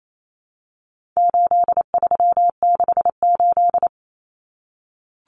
Morse code, heard as a keyed tone, about 28 wpm, 700 Hz. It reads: 8368
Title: Numbers - 4-Digit 28wpm